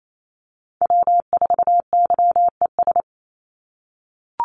Morse code, heard as a keyed tone, about 28 wpm, 700 Hz. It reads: W4YES
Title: Call Signs - 28wpm